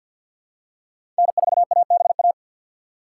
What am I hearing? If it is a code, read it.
Morse code, heard as a keyed tone, about 53 wpm, 700 Hz. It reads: N4ABA
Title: USA Call Signs - 5 Characters - 3x Character Spacing - 50wpm